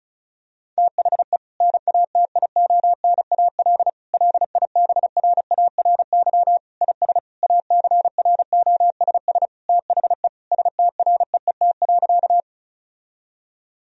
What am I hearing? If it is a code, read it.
Morse code, heard as a keyed tone, about 36 wpm, 700 Hz. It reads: THE NATIONAL LIBRARY IS ACROSS THE STREET.
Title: Sentences from Top 1000 Words - Review Part 4 - 35wpm